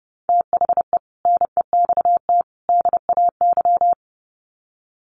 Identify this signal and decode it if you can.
Morse code, heard as a keyed tone, about 29 wpm, 700 Hz. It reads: THE NEXT DAY
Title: Sets of 3 Words 30wpm